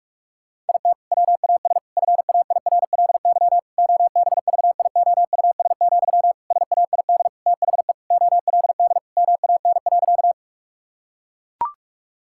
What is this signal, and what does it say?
Morse code, heard as a keyed tone, about 41 wpm, 700 Hz. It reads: IT WAS FAIRLY OBVIOUS, SAID THE OLD MAN.
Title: Sentences from Top 1600 Words 45wpm